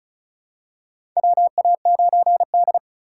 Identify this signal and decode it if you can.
Morse code, heard as a keyed tone, about 36 wpm, 700 Hz. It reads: WA9D
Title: USA Call Signs - 4 Characters - 4x Character Spacing - 35wpm